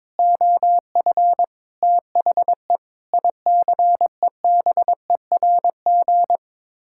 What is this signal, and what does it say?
Morse code, heard as a keyed tone, about 21 wpm, 700 Hz. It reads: OF THE ICEBERG
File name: practice / 22wpm / Sets of 6 Words - 22wpm